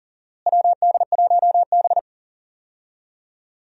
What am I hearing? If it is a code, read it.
Morse code, heard as a keyed tone, about 37 wpm, 700 Hz. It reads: WD1B
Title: USA Call Signs - 4 Characters - Repeated 2x - 40wpm